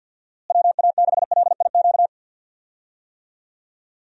Morse code, heard as a keyed tone, about 52 wpm, 700 Hz. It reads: WA6LIQ
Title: USA Call Signs - 6 Characters - Repeated 2x - 50wpm